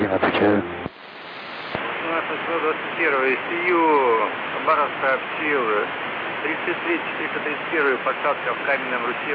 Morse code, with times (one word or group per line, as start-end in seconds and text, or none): none